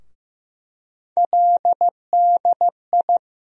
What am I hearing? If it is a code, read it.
Morse code, heard as a keyed tone, about 15 wpm, 700 Hz. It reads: LDI
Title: USA Call Sign Postfixes - 3 Letters 15wpm